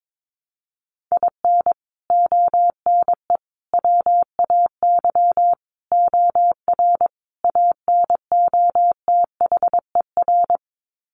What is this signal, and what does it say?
Morse code, heard as a keyed tone, about 22 wpm, 700 Hz. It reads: IN ONE WAY OR ANOTHER
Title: Sets of 5 Words Encore 22wpm